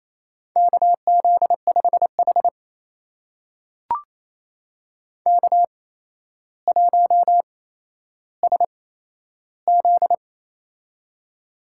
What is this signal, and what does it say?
Morse code, heard as a keyed tone, about 27 wpm, spaced as for 12 wpm, 700 Hz. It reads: KZ5H K1SZ
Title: USA Call Signs - 4 Characters - 8x Character Spacing - 28wpm